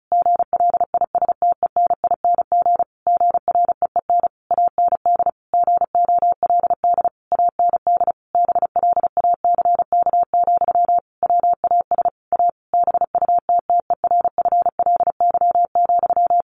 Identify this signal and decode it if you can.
Morse code, heard as a keyed tone, about 34 wpm, 700 Hz. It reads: GLISTENING GREEN AND GOLD AND BLACK, WAS A BUTTERFLY,
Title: A Sound Of Thunder 35wpm